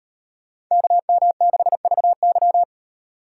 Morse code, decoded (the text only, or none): KM6VY